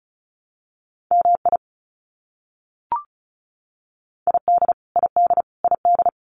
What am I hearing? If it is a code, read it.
Morse code, heard as a keyed tone, about 34 wpm, 700 Hz. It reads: MI ID ID ID
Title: US States Repeated 3x 35wpm